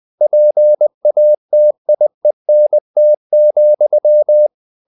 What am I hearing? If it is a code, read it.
Morse code, heard as a keyed tone, about 20 wpm, 600 Hz. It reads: PATIENT,
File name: the old man and the sea / the old man and the sea part 1 20wpm